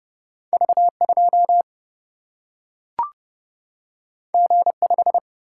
Morse code, heard as a keyed tone, about 30 wpm, 700 Hz. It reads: V2 G5
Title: Single-Letter Single-Number 30wpm